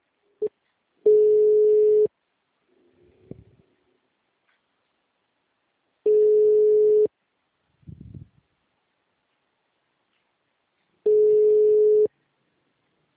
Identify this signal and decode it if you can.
Ringback tone (Europe, 425 Hz).
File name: out- -2006- - - .640